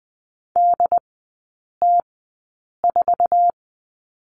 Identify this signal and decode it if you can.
Morse code, heard as a keyed tone, about 20 wpm, spaced as for 8 wpm, 700 Hz. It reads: DT4
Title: 114 Lesson 19e - Letter w - Three Characters 20wpm